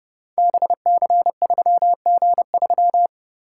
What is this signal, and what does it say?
Morse code, heard as a keyed tone, about 30 wpm, 700 Hz. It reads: BC3G3
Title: Code Groups - 5 Letter:Numbers - 30wpm